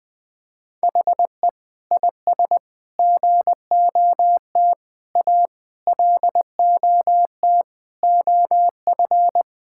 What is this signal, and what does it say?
Morse code, heard as a keyed tone, about 21 wpm, 700 Hz. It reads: HE IS GOT A LOT OF
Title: Sentences with 4-Word Phrases - 20wpm